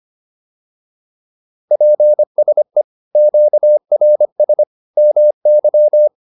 Morse code, heard as a keyed tone, about 25 wpm, 600 Hz. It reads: PSE QRS MY
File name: QSO210a 25